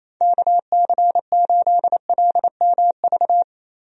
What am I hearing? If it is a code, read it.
Morse code, heard as a keyed tone, about 28 wpm, 700 Hz. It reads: KC8LMV
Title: USA Call Signs - 6 Characters - Repeated 2x - 28wpm